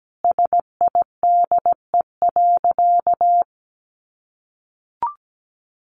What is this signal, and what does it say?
Morse code, heard as a keyed tone, about 17 wpm, 700 Hz. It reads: SIDE.
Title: Sentences from Top 1200 Words 17wpm